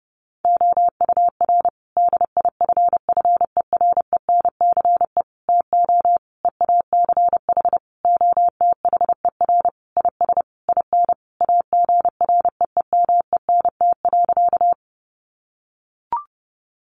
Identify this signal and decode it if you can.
Morse code, heard as a keyed tone, about 30 wpm, 700 Hz. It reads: OUR DIFFERENCE TO EACH OTHER IS IN AGREEMENT.
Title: Sentences from Top 1500 Words 30wpm